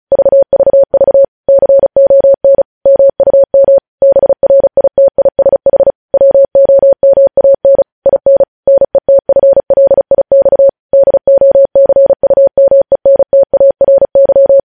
Morse code, heard as a keyed tone, about 36 wpm, 550 Hz. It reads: VVV CON MUM BRITISH WOMAN IN NETFLIX DOCUMENTARY
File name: podcast / News headlines-1x-35x35